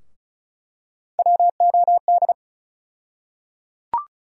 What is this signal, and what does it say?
Morse code, heard as a keyed tone, about 34 wpm, 700 Hz. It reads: WOD
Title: USA Call Sign Postfixes - 3 Letters 35wpm